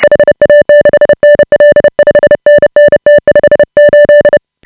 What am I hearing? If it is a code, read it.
Morse code, heard as a keyed tone, about 30 wpm, 590 Hz. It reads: HA6NL 5NN T58
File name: AUDIO2018 / HA5BA CW